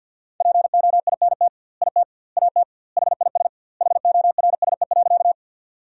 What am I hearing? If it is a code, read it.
Morse code, heard as a keyed tone, about 51 wpm, 700 Hz. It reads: POINT IT AT HIS HORSE.
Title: 2x Word-Spacing - Sentences from Top 300 Words - 50wpm